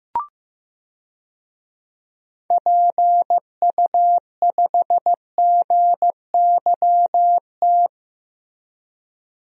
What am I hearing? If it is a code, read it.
Morse code, heard as a keyed tone, about 15 wpm, 700 Hz. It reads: PU5GYT
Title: DXCC-100 Call Signs - 6 Characters - Repeated 2x - 15wpm